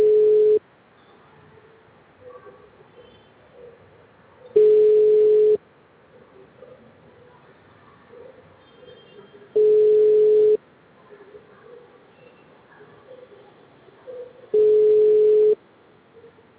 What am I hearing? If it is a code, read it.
Ringback tone (Europe, 425 Hz).